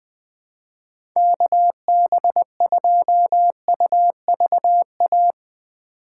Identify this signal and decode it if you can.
Morse code, heard as a keyed tone, about 21 wpm, 700 Hz. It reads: KB2UVA
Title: USA Call Signs - 6 Characters - 2x Character Spacing - 20wpm